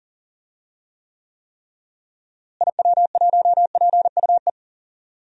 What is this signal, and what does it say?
Morse code, heard as a keyed tone, about 40 wpm, 700 Hz. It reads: IW1PUE
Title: DXCC-100 Call Signs - 6 Characters - Repeated 3x - 40wpm